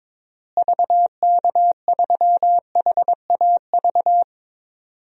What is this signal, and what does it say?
Morse code, heard as a keyed tone, about 21 wpm, 700 Hz. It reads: VK3HAV
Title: DXCC-100 Call Signs - 6 Characters - Repeated 2x - 22wpm